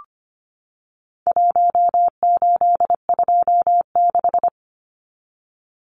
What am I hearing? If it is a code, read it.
Morse code, heard as a keyed tone, about 25 wpm, 700 Hz. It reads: 1826
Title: Numbers - Contest Style - 4-Digit 25wpm